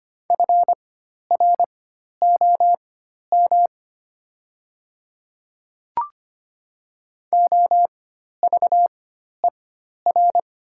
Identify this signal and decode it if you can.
Morse code, heard as a keyed tone, about 25 wpm, spaced as for 12 wpm, 700 Hz. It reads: FROM OVER
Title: 4x Character Spacing - Top 100 Words - 25wpm